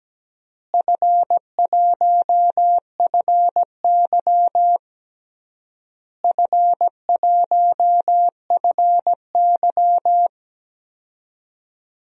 Morse code, heard as a keyed tone, about 17 wpm, 700 Hz. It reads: F1FY F1FY
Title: DXCC-100 Call Signs - 4 Characters - Repeated 3x - 17wpm